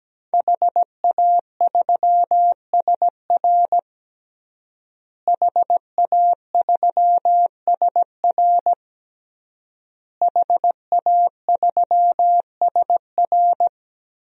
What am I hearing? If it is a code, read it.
Morse code, heard as a keyed tone, about 17 wpm, 700 Hz. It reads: HA3SR HA3SR HA3SR
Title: DXCC-100 Call Signs - 5 Characters - Repeated 3x - 17wpm